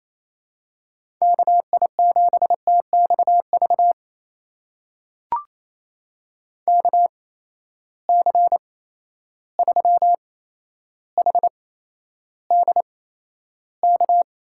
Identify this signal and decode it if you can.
Morse code, heard as a keyed tone, about 27 wpm, spaced as for 12 wpm, 700 Hz. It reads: KI7TXV KC3HDK
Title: USA Call Signs - 6 Characters - 8x Character Spacing - 28wpm